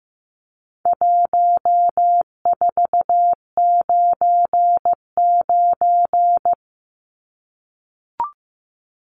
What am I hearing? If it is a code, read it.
Morse code, heard as a keyed tone, about 15 wpm, 700 Hz. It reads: 1499
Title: Numbers - 4-Digit 15wpm